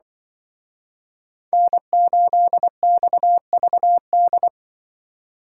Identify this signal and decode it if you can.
Morse code, heard as a keyed tone, about 25 wpm, 700 Hz. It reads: N8XVD
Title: Speed Racing - USA 1x3 Call Signs 20wpm